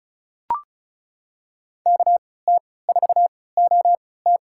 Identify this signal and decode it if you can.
Morse code, heard as a keyed tone, about 36 wpm, spaced as for 20 wpm, 700 Hz. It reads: KT4OT
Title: USA Call Signs - 5 Characters - 3x Character Spacing - 35wpm